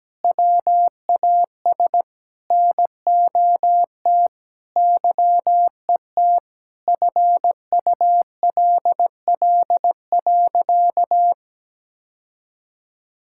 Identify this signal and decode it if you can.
Morse code, heard as a keyed tone, about 17 wpm, 700 Hz. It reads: WAS NOT YET FULL.
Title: Sentences from Top 1600 Words 17wpm